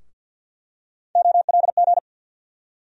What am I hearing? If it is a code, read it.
Morse code, heard as a keyed tone, about 48 wpm, 700 Hz. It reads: OLG
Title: USA Call Sign Postfixes - 3 Letters 50wpm